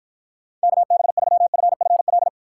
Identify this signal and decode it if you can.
Morse code, heard as a keyed tone, about 52 wpm, 700 Hz. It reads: KB3FFL